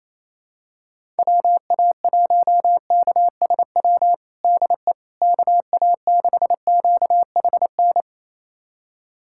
Morse code, heard as a keyed tone, about 28 wpm, 700 Hz. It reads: WA1KSW DE KA6QHN
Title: USA Callsign DE Callsign - 6 Characters - 28wpm